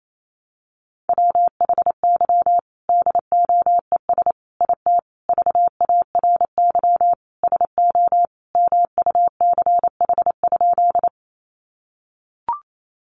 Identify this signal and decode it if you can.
Morse code, heard as a keyed tone, about 28 wpm, 700 Hz. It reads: WHY DOES IT VARY SO MUCH?
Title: Sentences from Top 1200 Words 28wpm